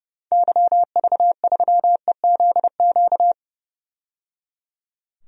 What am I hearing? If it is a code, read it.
Morse code, heard as a keyed tone, about 30 wpm, 700 Hz. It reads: YV3EZQ
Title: DXCC-100 Call Signs - 6 Characters - 30wpm